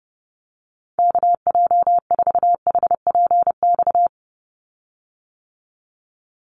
Morse code, heard as a keyed tone, about 30 wpm, 700 Hz. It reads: KJ4HPX
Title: USA Call Signs 30wpm